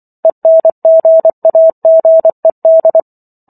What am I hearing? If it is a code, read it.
Morse code, heard as a keyed tone, about 24 wpm, 650 Hz. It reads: ENGAGED
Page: War of the Worlds - 15-Chapter 15 - 24 WPM